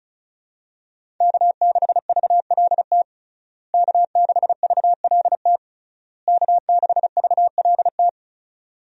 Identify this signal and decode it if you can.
Morse code, heard as a keyed tone, about 36 wpm, 700 Hz. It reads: K6VLT K6VLT K6VLT
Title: USA Call Signs - 5 Characters - Repeated 3x - 35wpm